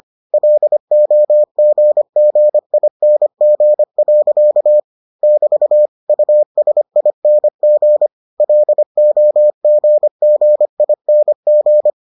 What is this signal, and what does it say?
Morse code, heard as a keyed tone, about 25 wpm, 600 Hz. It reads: LOGGING. <BT> USING LOGGING